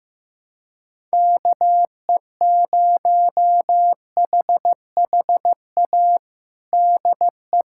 Morse code, heard as a keyed tone, about 16 wpm, 700 Hz. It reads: KE0HHA DE
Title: USA Callsign DE Callsign 15wpm